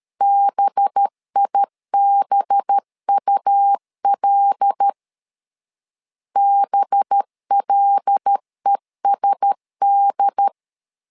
Morse code, heard as a keyed tone, about 13 wpm, 800 Hz. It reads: BIBUL BLESD
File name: LEZIONI / 5 LBVC / 5 LB 12wpm 1906